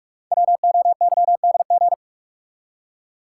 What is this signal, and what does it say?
Morse code, heard as a keyed tone, about 45 wpm, 690 Hz. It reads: WOYDG
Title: Code Groups - 5 Letters - 45wpm